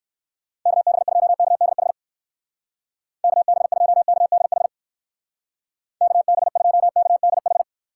Morse code, heard as a keyed tone, about 59 wpm, 700 Hz. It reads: KB2KDH KB2KDH KB2KDH